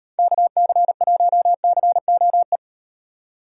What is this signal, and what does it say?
Morse code, heard as a keyed tone, about 35 wpm, 700 Hz. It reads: KC1COE